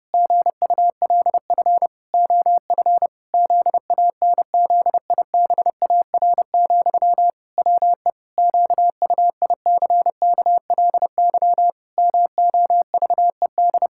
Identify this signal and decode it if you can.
Morse code, heard as a keyed tone, about 28 wpm, 700 Hz. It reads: GULF OF ZANZIBAR, WE QUICKLY MOVED
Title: Pangram Challenge v2 - 30wpm